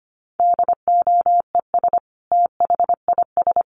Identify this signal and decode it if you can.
Morse code, heard as a keyed tone, about 25 wpm, 700 Hz. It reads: DOES THIS
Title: Sentences from Top 1500 Words 25wpm